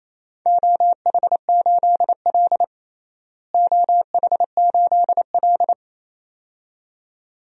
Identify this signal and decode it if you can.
Morse code, heard as a keyed tone, about 28 wpm, 700 Hz. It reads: OH8L OH8L
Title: DXCC-100 Call Signs - 4 Characters - Repeated 2x - 28wpm